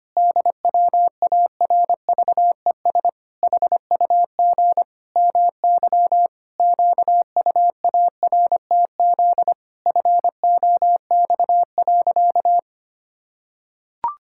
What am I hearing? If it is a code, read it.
Morse code, heard as a keyed tone, about 25 wpm, 700 Hz. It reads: DWARVES HUG MY QUARTZ FOX.
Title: Pangram Challenge v2 - 25wpm